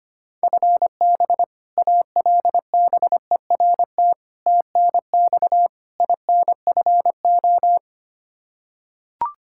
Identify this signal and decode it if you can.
Morse code, heard as a keyed tone, about 25 wpm, 700 Hz. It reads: FB ALBERT TNX INFO
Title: Practice QSO - Exchange 3 - More Info - 25wpm